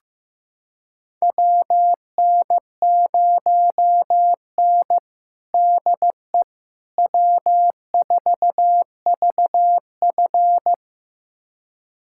Morse code, heard as a keyed tone, about 16 wpm, 700 Hz. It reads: WN0N DE W4VF
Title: USA Callsign DE Callsign - 4 Characters - 15wpm